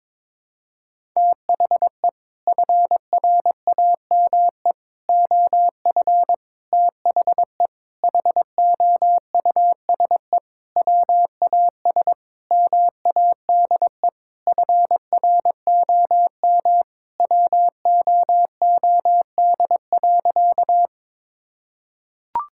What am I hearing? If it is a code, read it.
Morse code, heard as a keyed tone, about 22 wpm, 700 Hz. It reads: THE FRAME OF THE HOUSE WAS MADE FROM WOOD.
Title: Sentences from Top 1000 Words - Review Part 6 - 22wpm